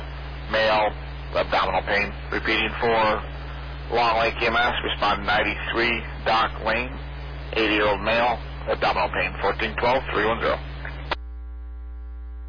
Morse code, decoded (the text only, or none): none